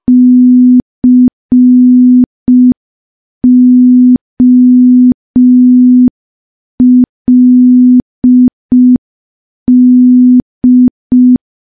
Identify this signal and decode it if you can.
Morse code, heard as a keyed tone, about 5 wpm, 250 Hz. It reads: COLD
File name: morse / morse-training-05WPM